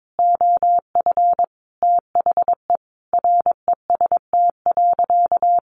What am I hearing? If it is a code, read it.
Morse code, heard as a keyed tone, about 22 wpm, 700 Hz. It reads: OF THE REST.
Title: Sentences from Top 1500 Words 22wpm